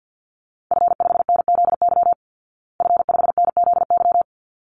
Morse code, heard as a keyed tone, about 50 wpm, 700 Hz. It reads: F5NZY F5NZY
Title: Call Signs Repeated 3x - 50wpm